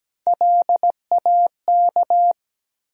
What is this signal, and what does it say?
Morse code, heard as a keyed tone, about 17 wpm, 700 Hz. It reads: LAK
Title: USA Call Sign Postfixes - 3 Letters 17wpm